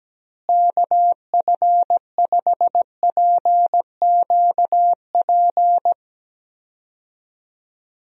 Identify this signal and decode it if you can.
Morse code, heard as a keyed tone, about 17 wpm, 700 Hz. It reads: KF5PQP
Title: USA Call Signs - 6 Characters - Repeated 2x - 17wpm